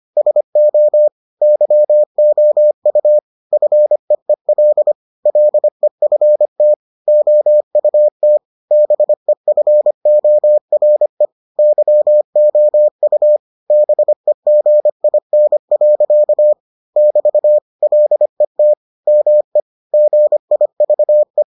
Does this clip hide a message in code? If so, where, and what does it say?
Yes: Morse code from 0.2 to 21.4 s, heard as a keyed tone, about 25 wpm, 600 Hz. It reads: SO YOU FEEL LEFT OUT BEFORE YOU BEGIN. <BT> LET ME GIVE